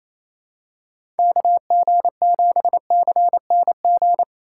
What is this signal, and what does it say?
Morse code, heard as a keyed tone, about 28 wpm, 700 Hz. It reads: KG7CNG
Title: USA Call Signs - 6 Characters - 4x Character Spacing - 28wpm